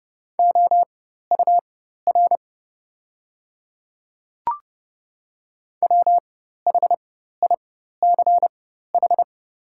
Morse code, heard as a keyed tone, about 30 wpm, spaced as for 14 wpm, 700 Hz. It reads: OUR WHICH
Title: 4x Character Spacing - Top 300 Words - 30wpm